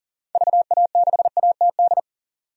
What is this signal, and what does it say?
Morse code, heard as a keyed tone, about 40 wpm, 700 Hz. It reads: VA6ATD
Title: DXCC-100 Call Signs - 6 Characters - 40wpm